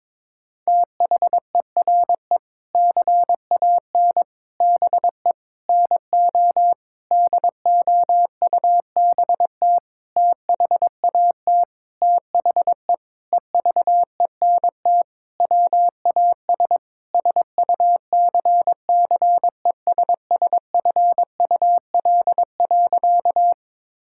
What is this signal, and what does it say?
Morse code, heard as a keyed tone, about 22 wpm, 700 Hz. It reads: THERE CAN BE NO DOUBT THAT THE EVENT WAS SUCCESSFUL.
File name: practice / 22wpm / Sentences with 7-Word Phrases - 22wpm